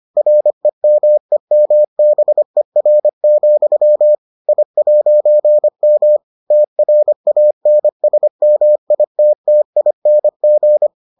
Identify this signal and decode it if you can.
Morse code, heard as a keyed tone, about 25 wpm, 600 Hz. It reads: REMEMBER, I'M TRANSMITTING